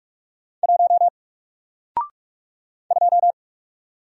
Morse code, heard as a keyed tone, about 45 wpm, 700 Hz. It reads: 1 2
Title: Single Character (Letter:Number:Punctuation:Prosign) Alt 45wpm